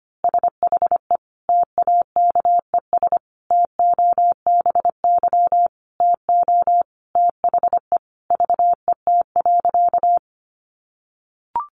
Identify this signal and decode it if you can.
Morse code, heard as a keyed tone, about 25 wpm, 700 Hz. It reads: SHE TAKES TOBY TO THE VET.
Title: Easy Phrases 25wpm